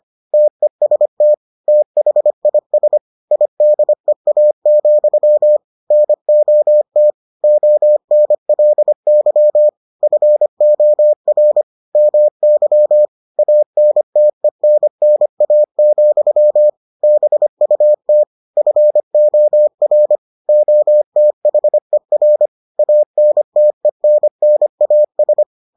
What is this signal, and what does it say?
Morse code, heard as a keyed tone, about 25 wpm, 600 Hz. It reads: TEST THIS IDEA, NOT ONLY FOR MY ANTENNA, BUT FOR OTHER ANTENNAS